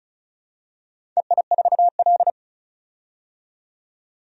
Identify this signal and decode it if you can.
Morse code, heard as a keyed tone, about 36 wpm, 700 Hz. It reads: EI4L
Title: DXCC-100 Call Signs - 4 Characters - 35wpm